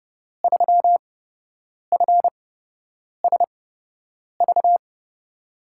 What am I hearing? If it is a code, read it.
Morse code, heard as a keyed tone, about 30 wpm, spaced as for 8 wpm, 700 Hz. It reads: 3FSV
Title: USA Call Signs - 5 Characters - 8x Character Spacing - 30wpm